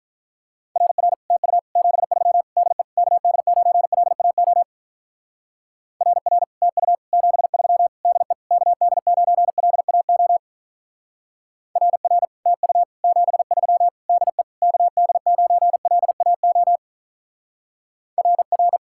Morse code, heard as a keyed tone, about 41 wpm, 700 Hz. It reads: RR TU 73 DE KD9LAO RR TU 73 DE KD9LAO RR TU 73 DE KD9LAO RR
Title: Speed-Racing Common QSO Phrases Encore - 35wpm